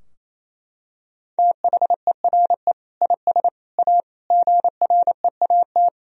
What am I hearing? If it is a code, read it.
Morse code, heard as a keyed tone, about 28 wpm, 700 Hz. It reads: THERE IS A GREAT
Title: Sentences from Top 1000 Words - Review Part 4 - 28wpm